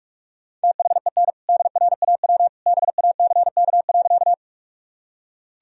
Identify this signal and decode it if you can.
Morse code, heard as a keyed tone, about 41 wpm, 700 Hz. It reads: THEN DRAW BACK.
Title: Sentences from Top 1000 Words - Review Part 1 - 45wpm